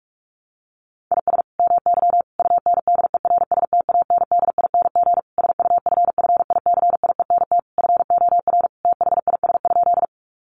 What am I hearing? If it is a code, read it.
Morse code, heard as a keyed tone, about 44 wpm, 700 Hz. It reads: IS MY UNDERSTANDING SUFFICIENT FOR THIS?
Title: Sentences from Top 1400 Words 45wpm